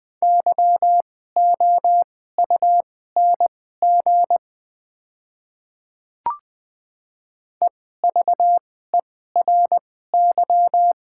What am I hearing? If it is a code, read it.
Morse code, heard as a keyed tone, about 20 wpm, spaced as for 14 wpm, 700 Hz. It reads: YOUNG EVERY
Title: 2x Character Spacing - Top 500 Words - 20wpm